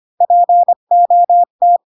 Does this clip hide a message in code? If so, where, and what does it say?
Yes: Morse code from 0.2 to 1.8 s, heard as a keyed tone, about 25 wpm, 700 Hz. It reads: POT